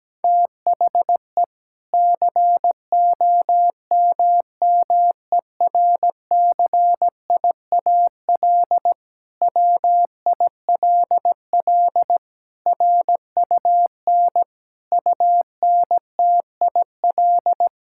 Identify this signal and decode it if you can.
Morse code, heard as a keyed tone, about 17 wpm, 700 Hz. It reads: THE COMMERCIAL WILL RUN UNTIL
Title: Sentences from Top 1000 Words - Review Part 5 - 17wpm